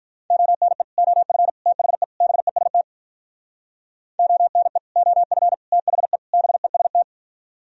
Morse code, heard as a keyed tone, about 46 wpm, 700 Hz. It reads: ONE OF THE BEST ONE OF THE BEST